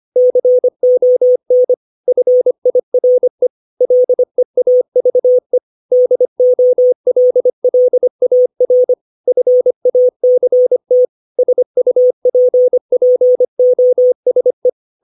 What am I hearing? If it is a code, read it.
Morse code, heard as a keyed tone, about 25 wpm, 500 Hz. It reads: CON FIRE LEAVE DOLLAR FACT SUPPOSE
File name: 632 words / 632 Words 25WPM